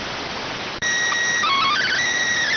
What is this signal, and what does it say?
SSTV (slow-scan television), PD 290.